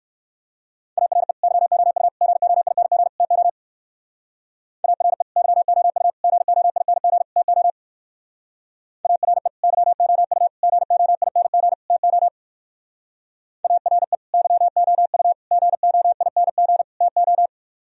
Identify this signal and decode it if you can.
Morse code, heard as a keyed tone, about 61 wpm, 700 Hz. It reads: ARE YOU GOING TO ARE YOU GOING TO ARE YOU GOING TO ARE YOU GOING TO